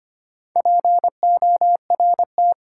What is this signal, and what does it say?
Morse code, heard as a keyed tone, about 25 wpm, 700 Hz. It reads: PORT
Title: Sentences from Top 1000 Words - Review Part 1 - New Word Drill - 25wpm